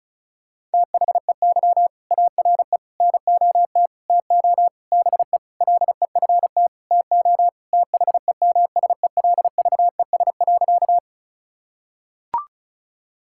Morse code, heard as a keyed tone, about 36 wpm, 700 Hz. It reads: THEY ARE NOT TO BE LEFT TO THEMSELVES.
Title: Sentences from Top 1000 Words - Review Part 5 - 35wpm